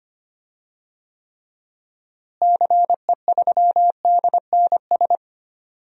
Morse code, heard as a keyed tone, about 25 wpm, 700 Hz. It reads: CE3DNS
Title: DXCC-100 Call Signs - 6 Characters - Repeated 2x - 25wpm